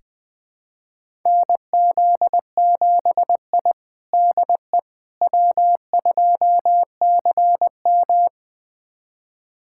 Morse code, heard as a keyed tone, about 20 wpm, 700 Hz. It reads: NZ7I DE W2CM
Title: USA Callsign DE Callsign - 4 Characters - 20wpm